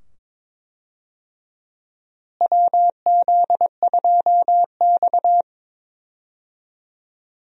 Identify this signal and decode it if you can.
Morse code, heard as a keyed tone, about 22 wpm, 700 Hz. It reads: WZ2X
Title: USA Call Signs - 4 Characters - Repeated 2x - 22wpm